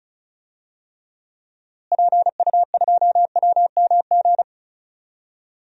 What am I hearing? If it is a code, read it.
Morse code, heard as a keyed tone, about 36 wpm, 700 Hz. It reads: PU2WMG
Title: DXCC-100 Call Signs - 6 Characters - Repeated 2x - 35wpm